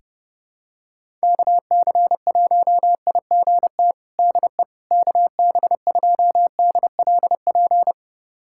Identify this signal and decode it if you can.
Morse code, heard as a keyed tone, about 31 wpm, 700 Hz. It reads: KC1IGT DE KB2DLP
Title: USA Callsign DE Callsign - 6 Characters - 30wpm